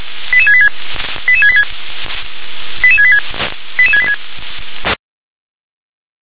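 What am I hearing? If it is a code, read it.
Five-tone selective call (ZVEI), heard as quick runs of five tones, tones of about 70 ms. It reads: 88686 88686 88686 88686
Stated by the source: Five-tone ZVEI selective call received by SDR.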